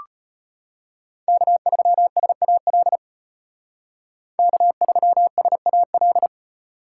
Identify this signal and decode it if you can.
Morse code, heard as a keyed tone, about 33 wpm, 700 Hz. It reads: K3SAL K3SAL